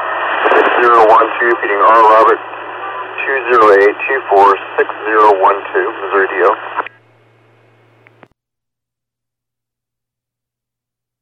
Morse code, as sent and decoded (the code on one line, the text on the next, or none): none